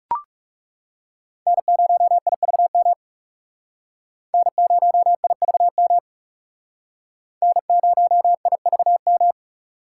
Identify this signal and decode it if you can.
Morse code, heard as a keyed tone, about 38 wpm, 700 Hz. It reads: N0IVM N0IVM N0IVM